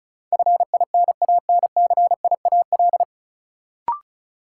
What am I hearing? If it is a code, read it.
Morse code, heard as a keyed tone, about 36 wpm, 700 Hz. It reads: FINANCIAL
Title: Sentences from Top 1000 Words - Review Part 5 - New Word Drill - 35wpm